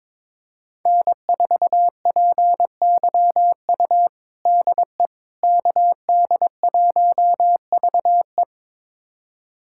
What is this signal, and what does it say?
Morse code, heard as a keyed tone, about 22 wpm, 700 Hz. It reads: N4PYU DE KD1VE
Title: USA Callsign DE Callsign - 5 Characters - 22wpm